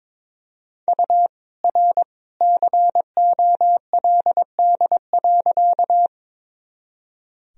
Morse code, heard as a keyed tone, about 22 wpm, 700 Hz. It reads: U R COLD.
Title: Sentences from Top 400 Words and Abbreviations 22wpm